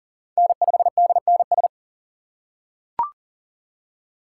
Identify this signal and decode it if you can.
Morse code, heard as a keyed tone, about 40 wpm, 700 Hz. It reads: N5DNS
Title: USA Call Signs - 3x Character Spacing - 40wpm